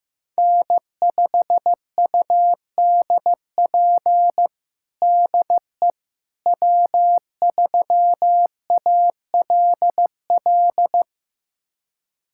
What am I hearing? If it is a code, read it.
Morse code, heard as a keyed tone, about 15 wpm, 700 Hz. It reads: N5UDP DE W3ALL
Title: USA Callsign DE Callsign - 5 Characters - 15wpm